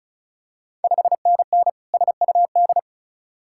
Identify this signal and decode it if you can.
Morse code, heard as a keyed tone, about 36 wpm, 700 Hz. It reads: 5NN SUD
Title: Ontario QSO Party Exchange 35wpm